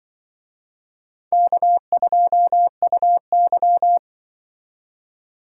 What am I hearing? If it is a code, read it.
Morse code, heard as a keyed tone, about 25 wpm, 700 Hz. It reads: K2UY
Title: Speed Racing - USA 1x2 Call Signs 20wpm